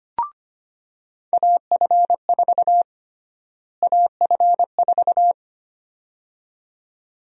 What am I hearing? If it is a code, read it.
Morse code, heard as a keyed tone, about 25 wpm, 700 Hz. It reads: AF4 AF4
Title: USA Call Sign Prefixes - Repeated 2x - 25wpm